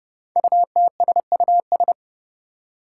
Morse code, heard as a keyed tone, about 30 wpm, 700 Hz. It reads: UTSUS
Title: Five Challenging Letter-Number Combinations 30wpm